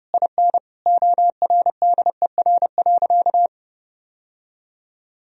Morse code, heard as a keyed tone, about 32 wpm, 700 Hz. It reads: IN ORDER.
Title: Sentences from Top 1000 Words - Review Part 2 - 30wpm